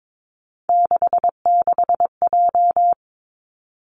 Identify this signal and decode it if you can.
Morse code, heard as a keyed tone, about 22 wpm, 700 Hz. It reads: 66J
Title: Three Challenging Letter-Number Combinations 22wpm